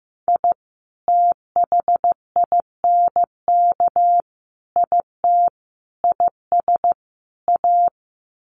Morse code, heard as a keyed tone, about 15 wpm, 700 Hz. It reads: I THINK IT IS A
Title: Sets of 5 Words Encore 15wpm